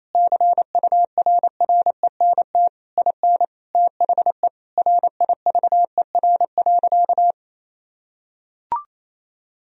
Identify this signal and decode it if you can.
Morse code, heard as a keyed tone, about 28 wpm, 700 Hz. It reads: CURRENT IN THE RIVER.
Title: Sentences from Top 1000 Words - Review Part 4 - 28wpm